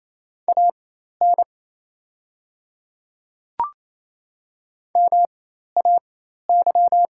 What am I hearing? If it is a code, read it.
Morse code, heard as a keyed tone, about 27 wpm, spaced as for 13 wpm, 700 Hz. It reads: AN MAY
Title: 4x Character Spacing - Top 100 Words - 28wpm